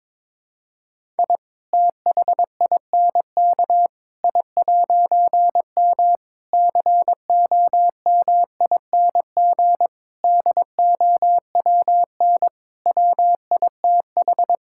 Morse code, heard as a keyed tone, about 22 wpm, 700 Hz. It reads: I THINK I'M COMING DOWN WITH
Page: morse-code-ninja-practice-22wpm